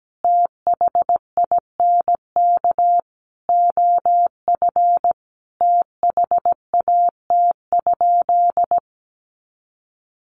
Morse code, heard as a keyed tone, about 17 wpm, 700 Hz. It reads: THINK OF THAT?
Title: Sentences from Top 200 Words 17wpm